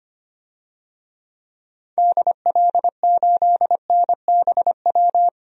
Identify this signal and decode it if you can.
Morse code, heard as a keyed tone, about 25 wpm, 700 Hz. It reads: DL8NBW
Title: DXCC-100 Call Signs - 6 Characters - 25wpm